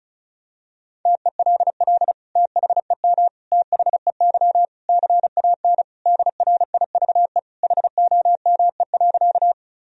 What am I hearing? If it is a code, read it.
Morse code, heard as a keyed tone, about 36 wpm, 700 Hz. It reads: TELL THEM THEY CAN DRIVE HOME.
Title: Sentences from Top 1000 Words - Review Part 2 - 35wpm